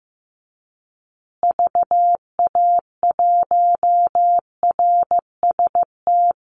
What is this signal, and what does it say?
Morse code, heard as a keyed tone, about 15 wpm, 700 Hz. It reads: VA1RST
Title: Call Signs Repeated 2x - 15wpm